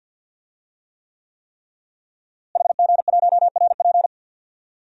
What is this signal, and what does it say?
Morse code, heard as a keyed tone, about 52 wpm, 700 Hz. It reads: HG1RP
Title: DXCC-100 Call Signs - 5 Characters - Repeated 3x - 50wpm